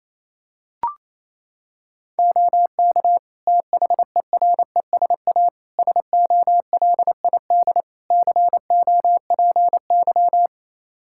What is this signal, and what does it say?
Morse code, heard as a keyed tone, about 28 wpm, 700 Hz. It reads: OK THERESA SOLID COPY
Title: Practice QSO - Exchange 3 - More Info - 28wpm